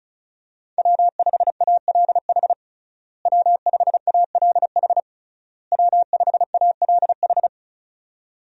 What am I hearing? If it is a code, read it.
Morse code, heard as a keyed tone, about 36 wpm, 700 Hz. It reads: W5ALH W5ALH W5ALH
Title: USA Call Signs - 5 Characters - Repeated 3x - 35wpm